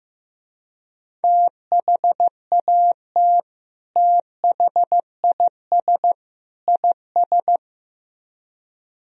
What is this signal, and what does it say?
Morse code, heard as a keyed tone, about 15 wpm, 700 Hz. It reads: THAT THIS IS
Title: Speed-Racing - Sets of 3 Words 15wpm